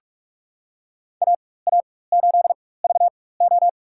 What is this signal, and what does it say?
Morse code, heard as a keyed tone, about 46 wpm, spaced as for 21 wpm, 700 Hz. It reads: AA8VO
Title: USA Call Signs - 5 Characters - 4x Character Spacing - 45wpm